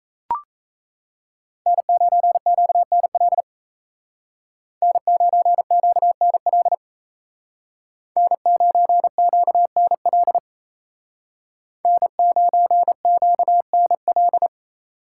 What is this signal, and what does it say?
Morse code, heard as a keyed tone, about 35 wpm, 700 Hz. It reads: N9QNL N9QNL N9QNL N9QNL
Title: Speed Racing - USA Call Signs 28wpm